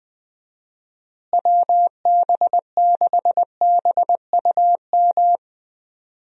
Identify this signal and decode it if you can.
Morse code, heard as a keyed tone, about 20 wpm, 700 Hz. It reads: WB6BUM
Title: USA Call Signs - 6 Characters - Repeated 2x - 20wpm